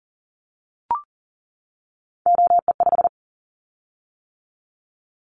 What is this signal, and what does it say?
Morse code, heard as a keyed tone, about 40 wpm, 700 Hz. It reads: OE5
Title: Callsign Prefixes for Top 25 DXCC - 40wpm